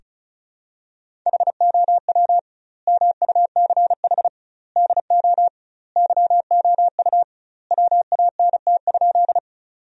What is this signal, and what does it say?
Morse code, heard as a keyed tone, about 34 wpm, 700 Hz. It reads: HOW MUCH DO YOU WANT?
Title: 2x Word-Spacing - Sentences from Top 200 Words - 35wpm